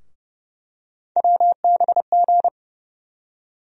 Morse code, heard as a keyed tone, about 30 wpm, 700 Hz. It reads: WBG
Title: USA Call Sign Postfixes - 3 Letters 30wpm